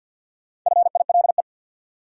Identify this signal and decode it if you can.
Morse code, heard as a keyed tone, about 47 wpm, 700 Hz. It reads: FILE
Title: Sentences from Top 1200 Words - New Word Drill Alt - 50wpm